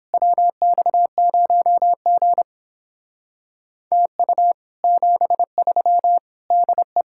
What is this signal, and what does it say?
Morse code, heard as a keyed tone, about 31 wpm, 700 Hz. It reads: WX0G TU 73 DE